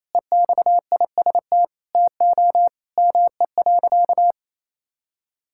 Morse code, heard as a keyed tone, about 28 wpm, 700 Hz. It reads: EXIST TO ME.
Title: Sentences from Top 1000 Words - Review Part 5 - 28wpm